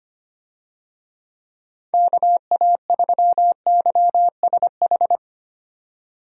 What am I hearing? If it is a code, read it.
Morse code, heard as a keyed tone, about 25 wpm, 700 Hz. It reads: KA3YSH
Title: USA Call Signs - 6 Characters - 25wpm